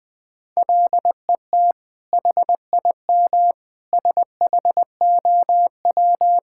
Morse code, heard as a keyed tone, about 20 wpm, 700 Hz. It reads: LET HIM SHOW
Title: Sentences from Top 1600 Words 20wpm